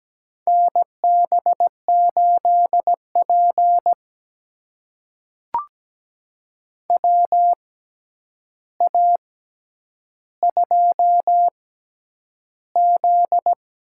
Morse code, heard as a keyed tone, about 17 wpm, spaced as for 9 wpm, 700 Hz. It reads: NB8P WA2Z
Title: USA Call Signs - 4 Characters - 6x Character Spacing - 17wpm